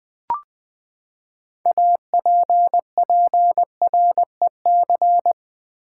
Morse code, heard as a keyed tone, about 20 wpm, 700 Hz. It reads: APPREC
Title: Build A Word - 6 Letters At A Time - 12-Letter Words - 20wpm